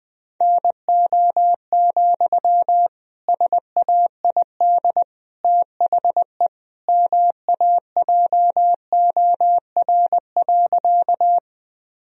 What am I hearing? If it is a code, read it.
Morse code, heard as a keyed tone, about 20 wpm, 700 Hz. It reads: NO, SAID THE MAJOR.
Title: Sentences from Top 1000 Words - Review Part 4 - 20wpm